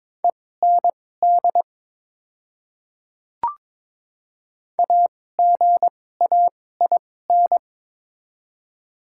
Morse code, heard as a keyed tone, about 21 wpm, spaced as for 16 wpm, 700 Hz. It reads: END AGAIN
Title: 2x Character Spacing - Top 200 Words - 22wpm